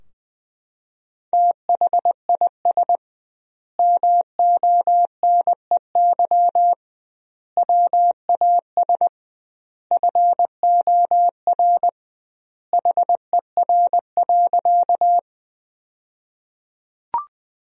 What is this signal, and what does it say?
Morse code, heard as a keyed tone, about 20 wpm, 700 Hz. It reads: THIS MONEY WAS FOR HER.
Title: 2x Word-Spacing - Sentences from Top 400 Words - 20wpm